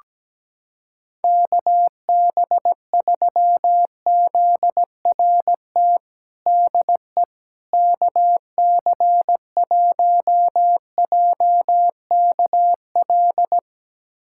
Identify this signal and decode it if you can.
Morse code, heard as a keyed tone, about 17 wpm, 700 Hz. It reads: KB3ZRT DE KC1JKL
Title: USA Callsign DE Callsign 17wpm